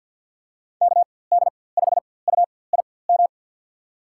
Morse code, heard as a keyed tone, about 47 wpm, spaced as for 23 wpm, 700 Hz. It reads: KD5UIM
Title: USA Call Signs - 6 Characters - 4x Character Spacing - 50wpm